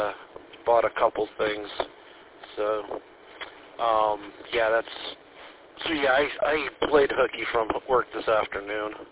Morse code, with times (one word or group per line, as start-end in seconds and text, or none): none